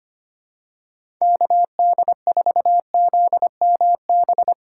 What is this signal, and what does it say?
Morse code, heard as a keyed tone, about 25 wpm, 700 Hz. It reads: KD4ZMB
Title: USA Call Signs - 6 Characters - 2x Character Spacing - 25wpm